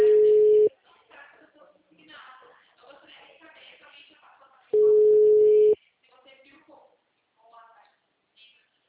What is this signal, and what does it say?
Ringback tone (Europe, 425 Hz), with speech.
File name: out- -2002- - - .41466